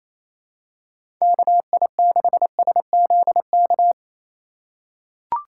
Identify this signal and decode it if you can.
Morse code, heard as a keyed tone, about 28 wpm, 700 Hz. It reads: KI6SZK
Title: USA Call Signs - 6 Characters - 2x Character Spacing - 28wpm